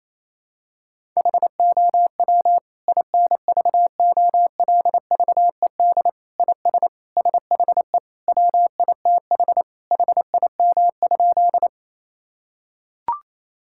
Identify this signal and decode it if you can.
Morse code, heard as a keyed tone, about 28 wpm, 700 Hz. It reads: HOW INVOLVED IS SHE WITH HIM?
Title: Sentences from Top 1000 Words - Review Part 5 - 28wpm